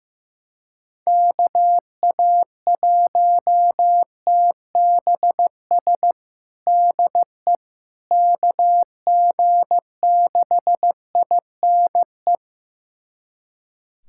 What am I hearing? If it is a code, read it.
Morse code, heard as a keyed tone, about 15 wpm, 700 Hz. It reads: KA1TBS DE KG6INE
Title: USA Callsign DE Callsign 15wpm